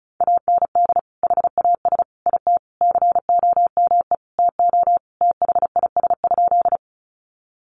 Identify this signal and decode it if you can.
Morse code, heard as a keyed tone, about 34 wpm, 700 Hz. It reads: AND HAS IT COME TO THIS?
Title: Sentences from Top 100 Words 35wpm